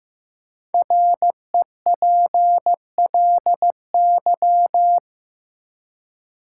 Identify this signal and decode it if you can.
Morse code, heard as a keyed tone, about 15 wpm, 700 Hz. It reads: REPLY
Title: Five Letter Words - 15wpm